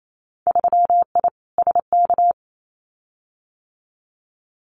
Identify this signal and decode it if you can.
Morse code, heard as a keyed tone, about 28 wpm, 700 Hz. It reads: 3I SK
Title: Winter Field Day Exchanges Encore 28wpm